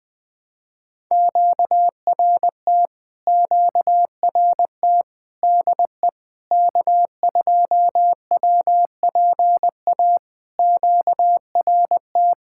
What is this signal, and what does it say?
Morse code, heard as a keyed tone, about 20 wpm, 700 Hz. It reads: QRT QRT DE K2WPA QRT
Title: POTA QSO Practice - 20wpm